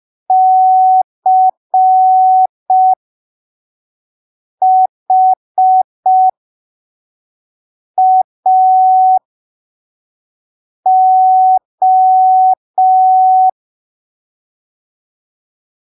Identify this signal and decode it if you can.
Morse code, heard as a keyed tone, about 5 wpm, spaced as for 3 wpm, 750 Hz. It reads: CHAO
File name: arquivos / texto / texto 21 05ppm